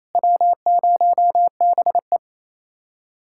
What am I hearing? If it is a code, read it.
Morse code, heard as a keyed tone, about 34 wpm, 700 Hz. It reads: W0BE
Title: Speed-Racing Common QSO Phrases - 28wpm